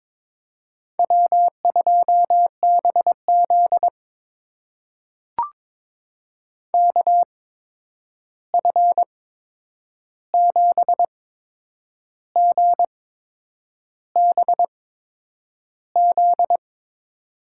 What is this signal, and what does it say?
Morse code, heard as a keyed tone, about 22 wpm, spaced as for 8 wpm, 700 Hz. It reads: W2BZ KF7GBZ
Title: USA Call Signs - 8x Character Spacing - 22wpm